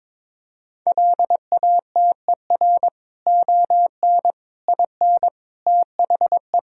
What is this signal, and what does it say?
Morse code, heard as a keyed tone, about 22 wpm, 700 Hz. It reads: LATER ON IN THE
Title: Sentences from Top 1000 Words - Review Part 4 - 22wpm